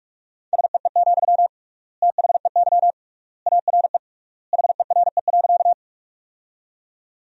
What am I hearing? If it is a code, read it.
Morse code, heard as a keyed tone, about 46 wpm, 700 Hz. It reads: SEE, THEY ARE HERE.
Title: 3x Word-Spacing - Sentences from Top 200 Words - 45wpm